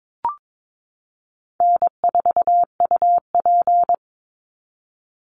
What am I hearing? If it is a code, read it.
Morse code, heard as a keyed tone, about 22 wpm, 700 Hz. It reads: N4UP
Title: Easy Call Signs Repeated 2x - 22wpm